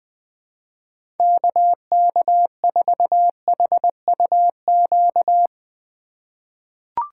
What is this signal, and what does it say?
Morse code, heard as a keyed tone, about 21 wpm, 700 Hz. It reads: KK4HUQ
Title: USA Call Signs - 6 Characters - 2x Character Spacing - 20wpm